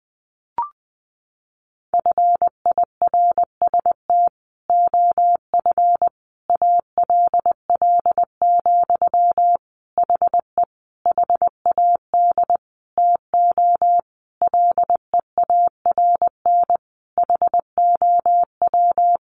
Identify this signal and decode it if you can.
Morse code, heard as a keyed tone, about 20 wpm, 700 Hz. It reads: FIRST OF ALL, HE HAD TO LEARN HOW
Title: Sentences from Top 1200 Words 20wpm